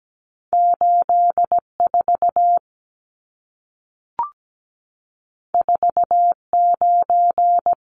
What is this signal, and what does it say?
Morse code, heard as a keyed tone, about 17 wpm, 700 Hz. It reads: 84 49
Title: Numbers - 2-Digit 17wpm